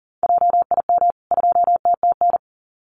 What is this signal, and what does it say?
Morse code, heard as a keyed tone, about 40 wpm, 700 Hz. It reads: JIM 2TTN
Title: CWT Exchange 40wpm